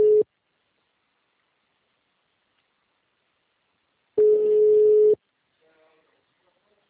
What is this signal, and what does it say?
Ringback tone (Europe, 425 Hz).